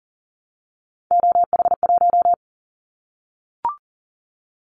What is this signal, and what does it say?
Morse code, heard as a keyed tone, about 40 wpm, 700 Hz. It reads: OH1
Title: Callsign Prefixes for Top 50 DXCC - 40wpm